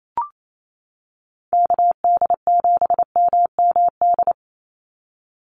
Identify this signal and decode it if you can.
Morse code, heard as a keyed tone, about 28 wpm, 700 Hz. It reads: KD7MMD
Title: USA Call Signs 28wpm